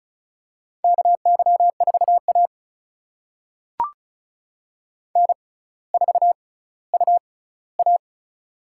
Morse code, heard as a keyed tone, about 36 wpm, spaced as for 18 wpm, 700 Hz. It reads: KY4A N4UA
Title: USA Call Signs - 4 Characters - 6x Character Spacing - 35wpm